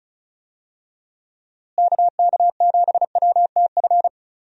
Morse code, heard as a keyed tone, about 34 wpm, 700 Hz. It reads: KK7WTF
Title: USA Call Signs - 6 Characters - Repeated 2x - 35wpm